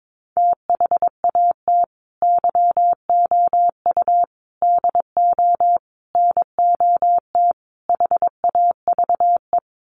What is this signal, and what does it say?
Morse code, heard as a keyed tone, about 22 wpm, 700 Hz. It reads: THAT YOU DO NOT HAVE
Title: Sets of 5 Words Encore 22wpm